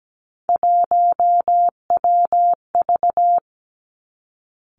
Morse code, heard as a keyed tone, about 17 wpm, 700 Hz. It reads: 1WV
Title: Three Challenging Letter-Number Combinations 17wpm